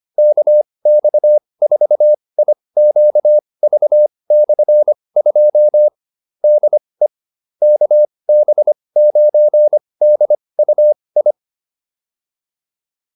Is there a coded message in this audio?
Yes: Morse code from 0.2 to 11.3 s, heard as a keyed tone, about 25 wpm, spaced as for 20 wpm, 600 Hz. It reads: KX4IQV/2 DE KB9DUI